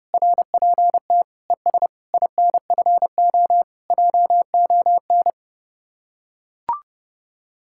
Morse code, heard as a keyed tone, about 30 wpm, 700 Hz. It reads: RPT ES INFO JON